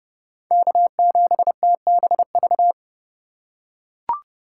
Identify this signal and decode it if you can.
Morse code, heard as a keyed tone, about 32 wpm, 700 Hz. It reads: K7TBV
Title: USA Call Signs - 5 Characters - 2x Character Spacing - 30wpm